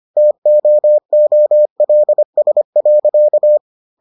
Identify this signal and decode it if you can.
Morse code, heard as a keyed tone, about 25 wpm, 600 Hz. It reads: TOOLS.